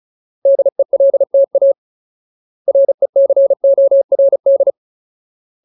Morse code, heard as a keyed tone, about 34 wpm, 550 Hz. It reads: DELTA RECORD
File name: Ultimate Word List / 35WPM Ultimate